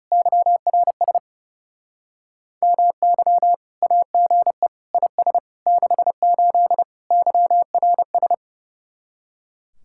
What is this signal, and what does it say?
Morse code, heard as a keyed tone, about 28 wpm, 700 Hz. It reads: YRS MY AGE IS 68 YRS
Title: Speed-Racing Common QSO Phrases - 30wpm